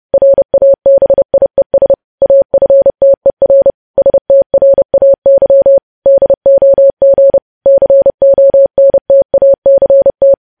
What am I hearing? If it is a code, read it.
Morse code, heard as a keyed tone, about 30 wpm, 550 Hz. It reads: RABIES AFTER STRAY DOG CONTACT
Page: news headlines in morse code at 30 wpm